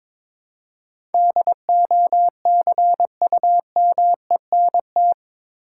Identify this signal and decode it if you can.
Morse code, heard as a keyed tone, about 22 wpm, 700 Hz. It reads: DOCUMENT
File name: practice / 22wpm / Eight Letter Words - 22wpm